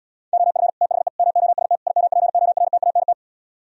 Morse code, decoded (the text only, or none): OF THE MOST IMPORTANT